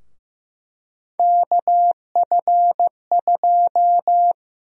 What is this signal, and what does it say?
Morse code, heard as a keyed tone, about 15 wpm, 700 Hz. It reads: KF2
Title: USA Call Sign Prefixes - 15wpm